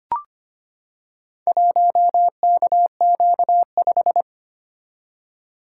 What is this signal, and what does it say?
Morse code, heard as a keyed tone, about 25 wpm, 700 Hz. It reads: 1KQ5
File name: practice / 25wpm / Four Letter-Number Combinations - 25wpm